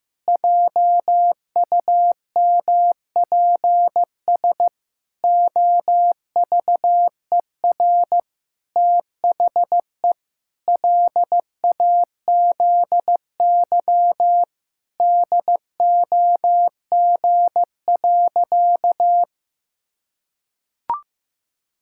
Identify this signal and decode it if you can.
Morse code, heard as a keyed tone, about 15 wpm, 700 Hz. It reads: JUMPS OVER THE LAZY DOG.
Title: Pangram Challenge v2 - 15wpm